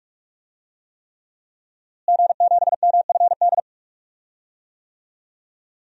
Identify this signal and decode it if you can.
Morse code, heard as a keyed tone, about 42 wpm, 700 Hz. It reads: G7MFD
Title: DXCC-100 Call Signs - 5 Characters - Repeated 3x - 45wpm